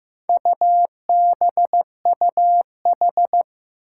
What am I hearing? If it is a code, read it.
Morse code, heard as a keyed tone, about 15 wpm, 700 Hz. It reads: UBUH
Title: Four Challenging Letter-Number Combinations 15wpm